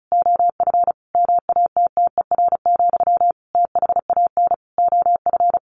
Morse code, heard as a keyed tone, about 34 wpm, 700 Hz. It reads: OF MATTER, THAN OF
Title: Sentences from Top 1400 Words 35wpm